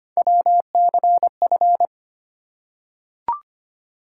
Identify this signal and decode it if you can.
Morse code, heard as a keyed tone, about 25 wpm, 700 Hz. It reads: WCF
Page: morse-code-ninja-practice-25wpm